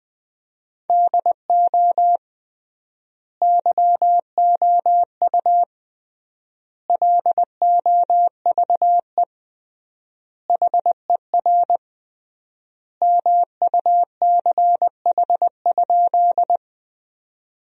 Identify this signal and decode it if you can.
Morse code, heard as a keyed tone, about 20 wpm, 700 Hz. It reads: DO YOU LOVE HER MUCH?
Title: 3x Word-Spacing - Sentences from Top 400 Words - 20wpm